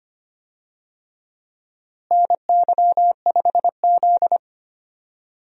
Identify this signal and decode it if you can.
Morse code, heard as a keyed tone, about 25 wpm, 700 Hz. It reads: NY5Z
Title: USA Call Signs - 4 Characters - Repeated 3x - 25wpm